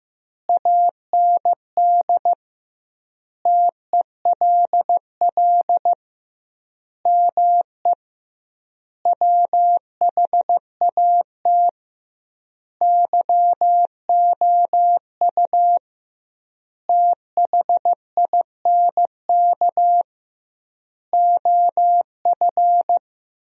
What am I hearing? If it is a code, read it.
Morse code, heard as a keyed tone, about 15 wpm, 700 Hz. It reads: AND TELL ME WHAT YOU THINK OF
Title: 2x Word-Spacing - Sentences from Top 200 Words - 15wpm